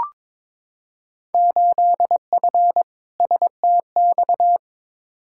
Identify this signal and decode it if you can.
Morse code, heard as a keyed tone, about 22 wpm, 700 Hz. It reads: 8F STX
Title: Field Day Exchanges Encore - 22wpm